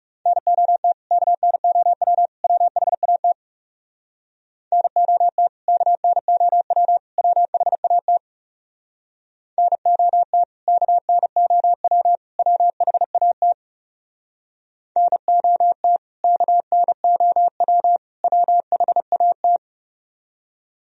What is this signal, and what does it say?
Morse code, heard as a keyed tone, about 38 wpm, 700 Hz. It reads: NOT KNOW WHAT NOT KNOW WHAT NOT KNOW WHAT NOT KNOW WHAT
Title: Speed-Racing - Sets of 3 Words 30wpm